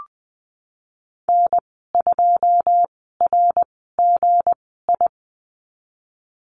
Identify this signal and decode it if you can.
Morse code, heard as a keyed tone, about 20 wpm, spaced as for 14 wpm, 700 Hz. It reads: N2RGI
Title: 161 Lesson 26g - Letter g - Call Signs 20wpm